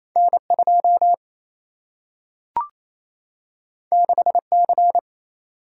Morse code, heard as a keyed tone, about 28 wpm, 700 Hz. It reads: N2 6C
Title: Single-Letter Single-Number 28wpm